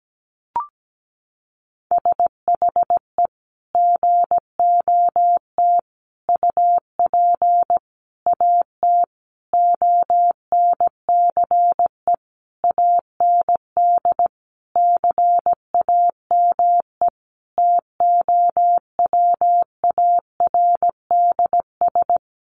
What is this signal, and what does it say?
Morse code, heard as a keyed tone, about 17 wpm, 700 Hz. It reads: SHE GOT UP AT ONCE AND CAME TOWARDS
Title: Sentences from Top 1100 Words 17wpm